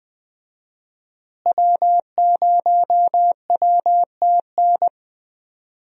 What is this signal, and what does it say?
Morse code, heard as a keyed tone, about 20 wpm, 700 Hz. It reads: W0WTN
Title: USA Call Signs - 5 Characters - Repeated 3x - 20wpm